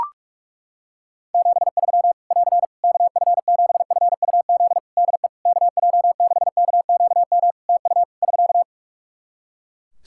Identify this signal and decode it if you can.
Morse code, heard as a keyed tone, about 43 wpm, 700 Hz. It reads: 73 <AR> KF7FUZ DE KJ6KQM TU <SK>
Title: Practice QSO - Exchange 4 - Ending - 45wpm